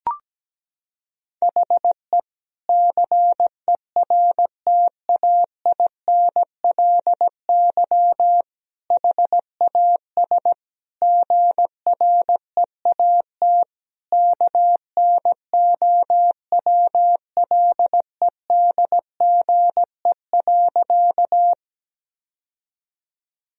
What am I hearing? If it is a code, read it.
Morse code, heard as a keyed tone, about 17 wpm, 700 Hz. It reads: HE CERTAINLY HAS GREAT KNOWLEDGE.
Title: Sentences from Top 1000 Words - Review Part 3 - 17wpm